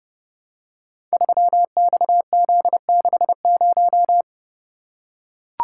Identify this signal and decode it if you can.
Morse code, heard as a keyed tone, about 30 wpm, 700 Hz. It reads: 3XZ60
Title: Code Groups - 5 Letter:Numbers - 30wpm